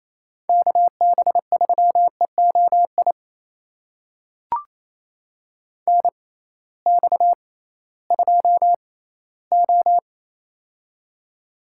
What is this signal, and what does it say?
Morse code, heard as a keyed tone, about 28 wpm, spaced as for 16 wpm, 700 Hz. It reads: KB3EOI NX2O
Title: USA Call Signs - 6x Character Spacing - 28wpm